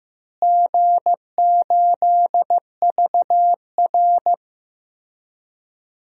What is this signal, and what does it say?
Morse code, heard as a keyed tone, about 15 wpm, 700 Hz. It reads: G8VR
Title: DXCC-100 Call Signs - 4 Characters - Repeated 3x - 15wpm